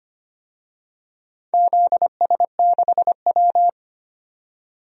Morse code, heard as a keyed tone, about 25 wpm, 700 Hz. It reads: ZS6W
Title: DXCC-100 Call Signs - 4 Characters - 25wpm